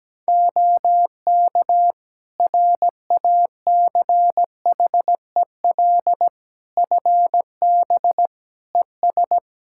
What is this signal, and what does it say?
Morse code, heard as a keyed tone, about 17 wpm, 700 Hz. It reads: OK RACHEL FB ES
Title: Practice QSO - Exchange 2 - Info - 17wpm